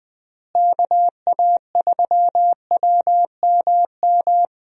Morse code, heard as a keyed tone, about 20 wpm, 700 Hz. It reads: KA3WMM
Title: USA Call Signs - 6 Characters - Repeated 2x - 20wpm